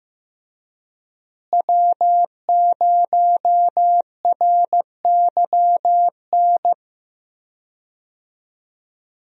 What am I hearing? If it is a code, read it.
Morse code, heard as a keyed tone, about 15 wpm, 700 Hz. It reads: W0RYN
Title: USA Call Signs - 5 Characters - Repeated 3x - 15wpm